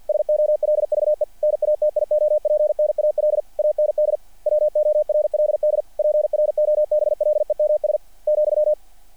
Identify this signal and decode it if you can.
Morse code, heard as a keyed tone, about 48 wpm, 600 Hz. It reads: SOLVE NATIOPNAL AND WORLD PROBLEMS ,
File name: archived / CRSnet